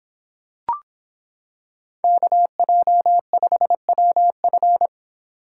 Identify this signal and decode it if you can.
Morse code, heard as a keyed tone, about 25 wpm, 700 Hz. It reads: KJ5WF